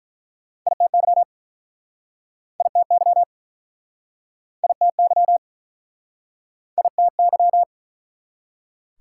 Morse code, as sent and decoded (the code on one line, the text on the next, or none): .. - -.-- / .. - -.-- / .. - -.-- / .. - -.--
ITY ITY ITY ITY